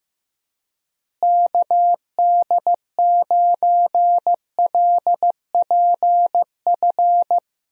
Morse code, heard as a keyed tone, about 16 wpm, 700 Hz. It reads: KD9LPF
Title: USA Call Signs - 6 Characters - Repeated 2x - 15wpm